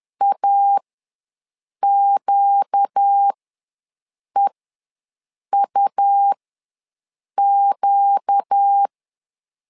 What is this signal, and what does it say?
Morse code, heard as a keyed tone, about 10 wpm, spaced as for 6 wpm, 800 Hz. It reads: AQEUQ
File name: LEZIONI / 4 Q ws 10wpm 2306